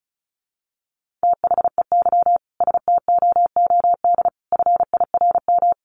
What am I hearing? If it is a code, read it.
Morse code, heard as a keyed tone, about 34 wpm, 700 Hz. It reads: THEY STOOD FIRM
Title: Sentences from Top 1400 Words 35wpm